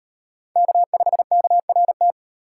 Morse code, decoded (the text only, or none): K5KRT